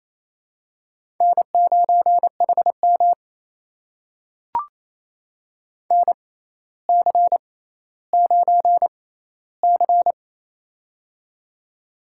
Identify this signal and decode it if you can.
Morse code, heard as a keyed tone, about 28 wpm, spaced as for 14 wpm, 700 Hz. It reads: N9HM NC9C
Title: USA Call Signs - 6x Character Spacing - 28wpm